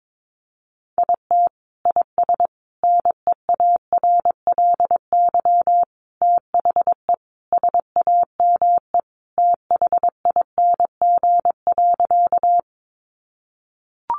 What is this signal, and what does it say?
Morse code, heard as a keyed tone, about 22 wpm, 700 Hz. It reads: IT IS NEARLY THE SAME THING.
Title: Sentences from Top 1300 Words 22wpm